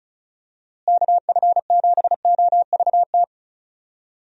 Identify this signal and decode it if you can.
Morse code, heard as a keyed tone, about 36 wpm, 700 Hz. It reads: KF7OVT
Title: USA Call Signs - 6 Characters - Repeated 2x - 35wpm